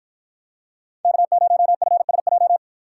Morse code, heard as a keyed tone, about 50 wpm, 700 Hz. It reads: K0FSJ